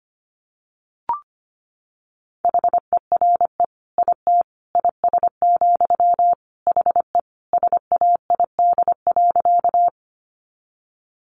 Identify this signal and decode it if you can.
Morse code, heard as a keyed tone, about 25 wpm, 700 Hz. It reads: HERE IT IS, HE SAID.
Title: Sentences from Top 200 Words 25wpm